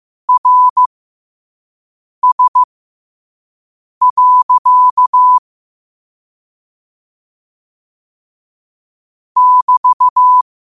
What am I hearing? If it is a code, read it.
Morse code, heard as a keyed tone, about 15 wpm, spaced as for 5 wpm, 1000 Hz. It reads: RS. <BT>
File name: qotdfiles / 5wpm